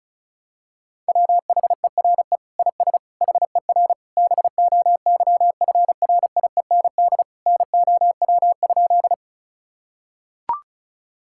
Morse code, heard as a keyed tone, about 36 wpm, 700 Hz. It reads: WHERE IS HER BOYFRIEND NOW?
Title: Sentences from Top 1000 Words - Review Part 5 - 35wpm